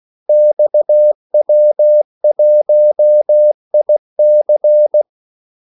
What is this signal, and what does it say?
Morse code, heard as a keyed tone, about 16 wpm, 600 Hz. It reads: XW1IC
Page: Kaldesignaler 16 wpm